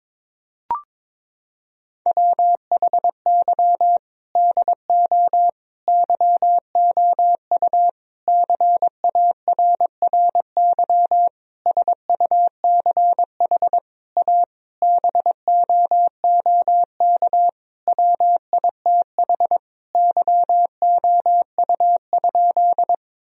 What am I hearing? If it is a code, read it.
Morse code, heard as a keyed tone, about 22 wpm, 700 Hz. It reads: WHY DO YOU CARRY SUCH A BOOK WITH YOU?
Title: Sentences from Top 1000 Words - Review Part 1 - 22wpm